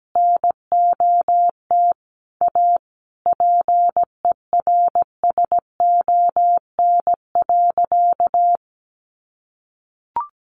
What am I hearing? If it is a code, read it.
Morse code, heard as a keyed tone, about 17 wpm, 700 Hz. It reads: NOT A PERSON.
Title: Sentences from Top 1500 Words 17wpm